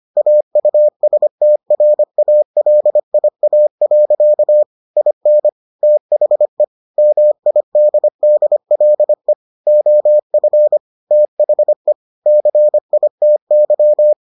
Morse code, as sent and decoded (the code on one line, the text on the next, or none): .- ..- ... - .-. .- .-.. .. .- .-.-.- / .. -. / - .... / -- .. -.. -.. .-.. / --- ..-. / - .... / -.-. .. - -.--
AUSTRALIA. IN THE MIDDLE OF THE CITY